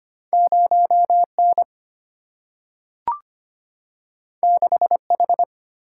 Morse code, heard as a keyed tone, about 25 wpm, 700 Hz. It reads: 0N 6H
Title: Single-Letter Single-Number 25wpm